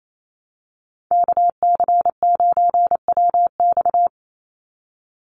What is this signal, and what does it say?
Morse code, heard as a keyed tone, about 28 wpm, 700 Hz. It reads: KC9WX
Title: USA Call Signs 28wpm